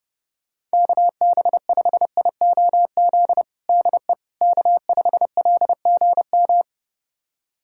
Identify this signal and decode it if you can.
Morse code, heard as a keyed tone, about 30 wpm, 700 Hz. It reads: KB5IOZ DE K5LGM
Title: USA Callsign DE Callsign 30wpm